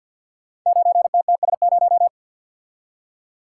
Morse code, heard as a keyed tone, about 52 wpm, 700 Hz. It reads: 9TTS0
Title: Code Groups - 5 Letter:Numbers - 50wpm